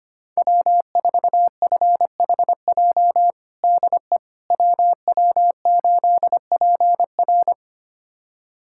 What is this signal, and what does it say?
Morse code, heard as a keyed tone, about 25 wpm, 700 Hz. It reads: W4FHJ DE WW8PR
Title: USA Callsign DE Callsign - 5 Characters - 25wpm